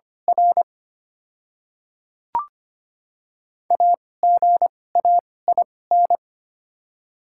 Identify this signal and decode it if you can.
Morse code, heard as a keyed tone, about 25 wpm, spaced as for 18 wpm, 700 Hz. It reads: R AGAIN
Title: 2x Character Spacing - Top 200 Words - 25wpm